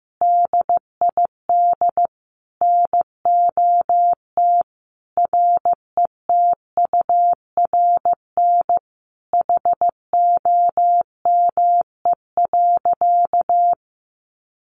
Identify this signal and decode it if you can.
Morse code, heard as a keyed tone, about 15 wpm, 700 Hz. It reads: DID NOT RETURN HOME.
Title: Sentences from Top 800 Words 15wpm